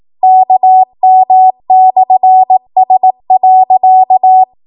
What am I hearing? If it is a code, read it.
Morse code, heard as a keyed tone, about 18 wpm, 750 Hz. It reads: KM/S.
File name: Morse / 18WPM